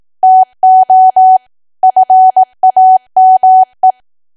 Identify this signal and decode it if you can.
Morse code, heard as a keyed tone, about 18 wpm, 750 Hz. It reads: TO FAME
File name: Archive / 18 WPM / 18WPM